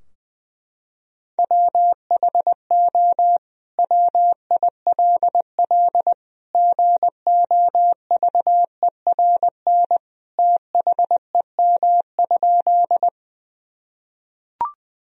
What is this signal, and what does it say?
Morse code, heard as a keyed tone, about 19 wpm, 700 Hz. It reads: WHO WILL GOVERN THEM?
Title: Sentences from Top 1000 Words - Review Part 2 - 20wpm